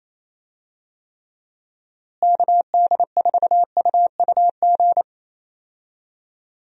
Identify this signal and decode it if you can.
Morse code, heard as a keyed tone, about 28 wpm, 700 Hz. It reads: KD4UUG
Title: USA Call Signs - 6 Characters - Repeated 2x - 28wpm